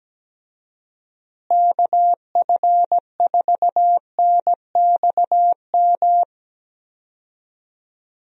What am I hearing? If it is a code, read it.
Morse code, heard as a keyed tone, about 17 wpm, 700 Hz. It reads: KF4NXM
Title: USA Call Signs - 6 Characters - Repeated 2x - 17wpm